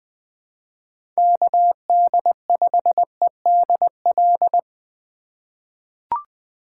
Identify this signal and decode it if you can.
Morse code, heard as a keyed tone, about 20 wpm, 700 Hz. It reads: KD5EDL
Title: 2023 POTA Hunter Call Signs - 20wpm